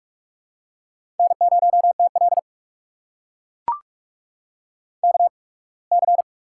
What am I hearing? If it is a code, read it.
Morse code, heard as a keyed tone, about 44 wpm, spaced as for 27 wpm, 700 Hz. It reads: N0TL KC
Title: USA Call Signs - 8x Character Spacing - 45wpm